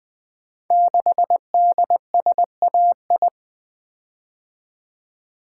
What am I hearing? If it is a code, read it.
Morse code, heard as a keyed tone, about 20 wpm, 700 Hz. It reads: 6DSAI
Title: Code Groups - 5 Letter:Numbers - 20wpm